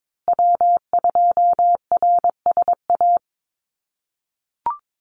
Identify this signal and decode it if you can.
Morse code, heard as a keyed tone, about 22 wpm, 700 Hz. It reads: W2RSA
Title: USA Call Signs Repeated 3x 22wpm